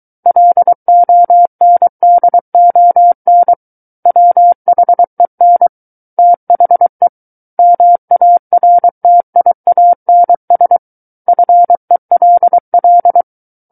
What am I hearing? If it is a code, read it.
Morse code, heard as a keyed tone, about 23 wpm, 700 Hz. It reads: LONDON WHEN THE MARTIANS FELL
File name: Book One / 14-Chapter 14 - 23 WPM